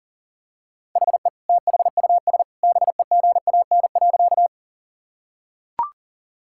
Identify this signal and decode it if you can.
Morse code, heard as a keyed tone, about 40 wpm, 700 Hz. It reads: HE THUS BEGAN.
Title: Sentences from Top 1000 Words - Review Part 5 - 40wpm